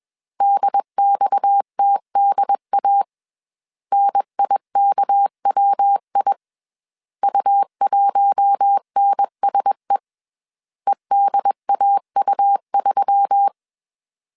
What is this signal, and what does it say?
Morse code, heard as a keyed tone, about 21 wpm, 800 Hz. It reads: DXTDA NIKWI U1NSE EDAU3
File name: CW / DET / 1 6 1500C S C 20wpm 2116